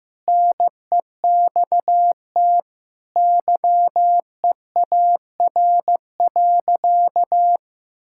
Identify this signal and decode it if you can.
Morse code, heard as a keyed tone, about 16 wpm, 700 Hz. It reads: NEXT YEAR.
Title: Sentences from Top 1000 Words - Review Part 6 - 15wpm